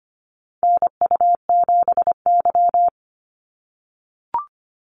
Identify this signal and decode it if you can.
Morse code, heard as a keyed tone, about 25 wpm, 700 Hz. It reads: NU7Y
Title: NA SOTA Chaser Callsigns 25wpm